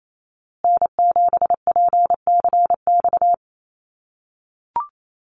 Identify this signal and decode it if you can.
Morse code, heard as a keyed tone, about 28 wpm, 700 Hz. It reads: N7PCX
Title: USA Call Signs Repeated 3x 28wpm